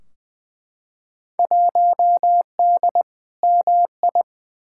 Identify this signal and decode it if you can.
Morse code, heard as a keyed tone, about 21 wpm, 700 Hz. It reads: 1D MI
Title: Field Day Exchanges - 20wpm